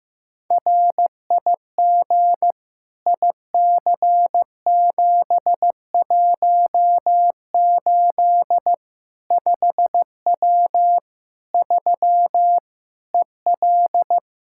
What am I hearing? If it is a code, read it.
Morse code, heard as a keyed tone, about 15 wpm, 700 Hz. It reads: RIG IC718 5W 3 EL
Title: K1USN SST QSO Practice 15wpm